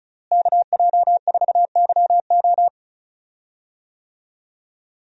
Morse code, heard as a keyed tone, about 34 wpm, 700 Hz. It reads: KJ4YO
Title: USA Call Signs - 5 Characters - Repeated 2x - 35wpm